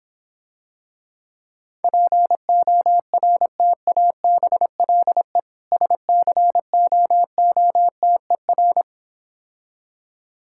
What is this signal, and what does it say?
Morse code, heard as a keyed tone, about 26 wpm, 700 Hz. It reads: PORTABLE SCOOTER